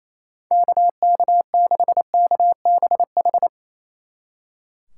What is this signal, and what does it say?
Morse code, heard as a keyed tone, about 36 wpm, 700 Hz. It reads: KK6KBH